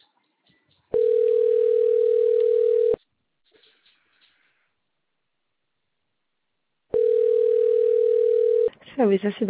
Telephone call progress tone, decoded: ringback tone